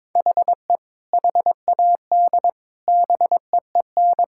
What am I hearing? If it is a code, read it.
Morse code, heard as a keyed tone, about 22 wpm, 700 Hz. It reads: HE HAD BEEN
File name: practice / 22wpm / Sentences with 3-Word Phrases - 22wpm